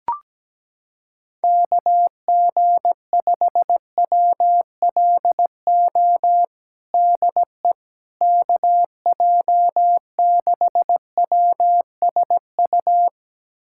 Morse code, heard as a keyed tone, about 17 wpm, 700 Hz. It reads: KG5WLO DE KJ6WSU
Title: USA Callsign DE Callsign 17wpm